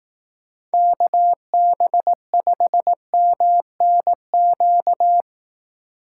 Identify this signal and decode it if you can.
Morse code, heard as a keyed tone, about 18 wpm, 700 Hz. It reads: KB5MNQ
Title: Speed Racing - USA 2x3 Call Signs 15wpm